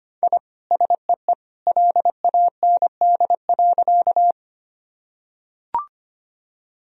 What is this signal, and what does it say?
Morse code, heard as a keyed tone, about 25 wpm, 700 Hz. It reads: I SEE LAND.
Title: Sentences from Top 1000 Words - Review Part 1 - 25wpm